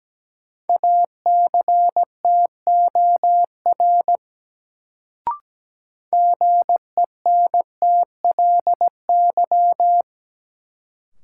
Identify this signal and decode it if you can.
Morse code, heard as a keyed tone, about 17 wpm, 700 Hz. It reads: ACTOR GENTLY
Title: Sentences from Top 1500 Words - New Word Drill - 17wpm